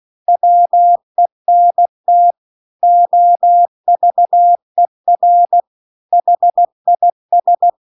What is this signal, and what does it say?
Morse code, heard as a keyed tone, about 16 wpm, 700 Hz. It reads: WENT OVER HIS
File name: CW-The-Big-Twoheaded-River-2-16WPM